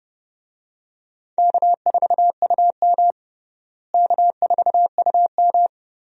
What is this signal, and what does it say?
Morse code, heard as a keyed tone, about 30 wpm, 700 Hz. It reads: K4UM K4UM
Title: USA Call Signs - 4 Characters - Repeated 3x - 30wpm